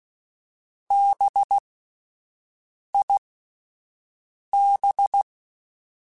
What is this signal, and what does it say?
Morse code, heard as a keyed tone, about 16 wpm, spaced as for 5 wpm, 800 Hz. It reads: BIB
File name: Alphabetical / Alphabetical mp3s / Bs @5